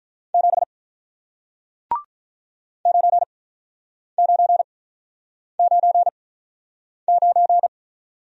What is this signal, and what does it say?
Morse code, heard as a keyed tone, about 47 wpm, spaced as for 9 wpm, 700 Hz. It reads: 7 9999